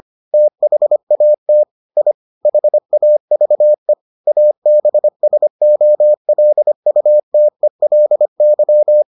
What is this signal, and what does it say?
Morse code, heard as a keyed tone, about 25 wpm, 600 Hz. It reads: THAT I HAVE ABSOLUTELY